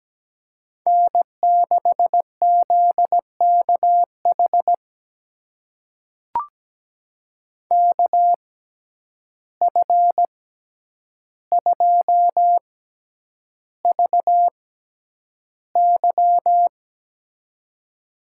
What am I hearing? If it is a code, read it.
Morse code, heard as a keyed tone, about 17 wpm, spaced as for 9 wpm, 700 Hz. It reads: N6ZKH KF2VY
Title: USA Call Signs - 5 Characters - 6x Character Spacing - 17wpm